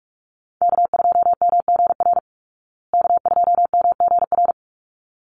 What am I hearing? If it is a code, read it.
Morse code, heard as a keyed tone, about 44 wpm, 700 Hz. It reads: K2MGR K2MGR
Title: Call Signs Repeated 3x - 45wpm